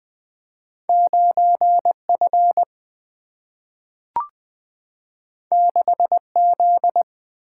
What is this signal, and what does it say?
Morse code, heard as a keyed tone, about 20 wpm, 700 Hz. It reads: 9 F 6 Z
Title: Two Letter-Number Combinations - 20wpm